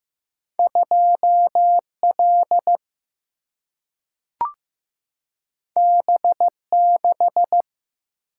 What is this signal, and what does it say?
Morse code, heard as a keyed tone, about 15 wpm, 700 Hz. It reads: 2L B6
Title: Single-Letter Single-Number 15wpm